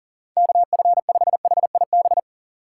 Morse code, heard as a keyed tone, about 37 wpm, 700 Hz. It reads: KF5HIB
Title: USA Call Signs - 6 Characters - 6x Character Spacing - 40wpm